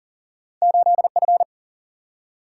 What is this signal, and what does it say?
Morse code, heard as a keyed tone, about 40 wpm, 700 Hz. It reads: 8 F
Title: Single-Letter Single-Number 40wpm